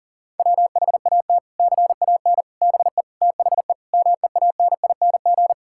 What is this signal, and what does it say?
Morse code, heard as a keyed tone, about 40 wpm, 700 Hz. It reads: WHAT CAN BE THE MEANING
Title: Sentences from Top 1000 Words - Review Part 6 - 40wpm